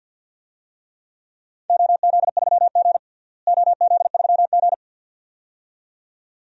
Morse code, heard as a keyed tone, about 52 wpm, 700 Hz. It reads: OZ3G OZ3G
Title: DXCC-100 Call Signs - 4 Characters - Repeated 2x - 50wpm